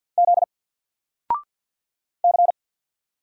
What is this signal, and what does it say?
Morse code, heard as a keyed tone, about 48 wpm, 700 Hz. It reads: Z C
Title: Single Letter-Number Alt 50wpm